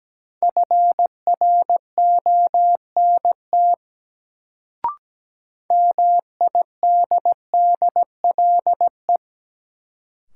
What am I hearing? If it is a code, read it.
Morse code, heard as a keyed tone, about 17 wpm, 700 Hz. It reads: FRONT MIDDLE
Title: Sets of 2 Words - New Word Drill - 17wpm